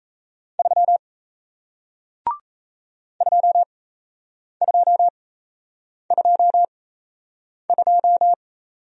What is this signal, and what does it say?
Morse code, heard as a keyed tone, about 37 wpm, spaced as for 8 wpm, 700 Hz. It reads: 3 2222